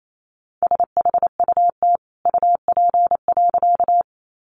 Morse code, heard as a keyed tone, about 28 wpm, 700 Hz. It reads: SHUT UP.
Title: A Sound Of Thunder 28wpm